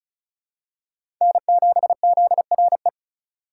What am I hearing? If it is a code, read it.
Morse code, heard as a keyed tone, about 34 wpm, 700 Hz. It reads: N7ZRE
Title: USA Call Signs - 5 Characters - 2x Character Spacing - 35wpm